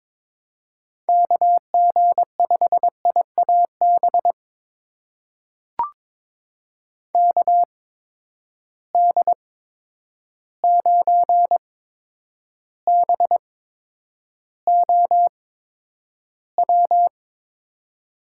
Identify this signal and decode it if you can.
Morse code, heard as a keyed tone, about 23 wpm, spaced as for 10 wpm, 700 Hz. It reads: KG5IAB KD9BOW
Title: USA Call Signs - 6 Characters - 8x Character Spacing - 22wpm